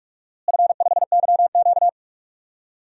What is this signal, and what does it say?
Morse code, heard as a keyed tone, about 45 wpm, 700 Hz. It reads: F5YQ
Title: DXCC-100 Call Signs - 4 Characters - Repeated 3x - 45wpm